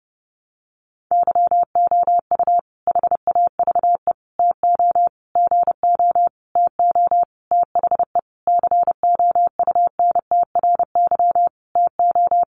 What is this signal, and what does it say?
Morse code, heard as a keyed tone, about 30 wpm, 700 Hz. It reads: YOU HAVE TO GO TO THE COUNTRY TO
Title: Sentences from Top 1300 Words 30wpm